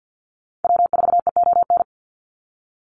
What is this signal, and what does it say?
Morse code, heard as a keyed tone, about 50 wpm, 700 Hz. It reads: W4EON
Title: NA SOTA Activator Callsigns 50wpm